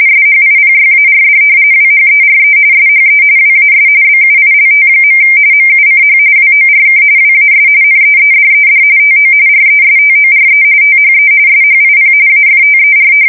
RTTY, about 45.45 baud. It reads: SHIPYARD WHERE SHE WAS BUILT. MUCH WORK WAS NEEDED TO RESTORE THE SHIP AFTER
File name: fsk170 / battleshipnj